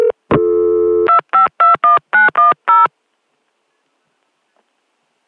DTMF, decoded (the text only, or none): 2521910